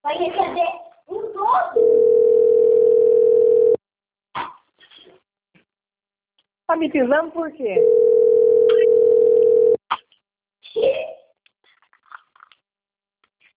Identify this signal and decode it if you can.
Ringback tone (North America), 440 + 480 Hz, with speech.
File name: rg-1600-unknown- - - .18954